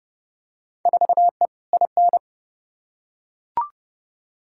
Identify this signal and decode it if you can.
Morse code, heard as a keyed tone, about 30 wpm, 700 Hz. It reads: 4E IN
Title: Field Day Exchanges - 30wpm